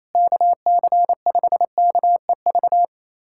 Morse code, heard as a keyed tone, about 28 wpm, 700 Hz. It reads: KC5KEV
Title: USA Callsign DE Callsign - 6 Characters - 28wpm